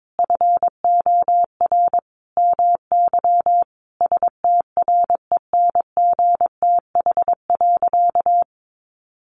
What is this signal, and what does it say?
Morse code, heard as a keyed tone, about 22 wpm, 700 Hz. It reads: FOR MY STRENGTH.
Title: Sentences from Top 1300 Words 22wpm